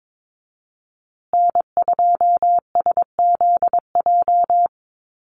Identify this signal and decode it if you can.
Morse code, heard as a keyed tone, about 22 wpm, 700 Hz. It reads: N2SZJ
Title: USA Call Signs Repeated 2x 22wpm